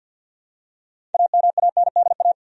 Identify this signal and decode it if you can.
Morse code, heard as a keyed tone, about 53 wpm, 700 Hz. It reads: AMANDA
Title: USA Common Names - 50wpm